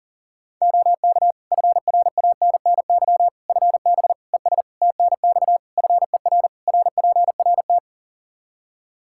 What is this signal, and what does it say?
Morse code, heard as a keyed tone, about 40 wpm, 700 Hz. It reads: OK FRANNY FB ES TNX FER RPRT
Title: Practice QSO - Exchange 2 - Info - 40wpm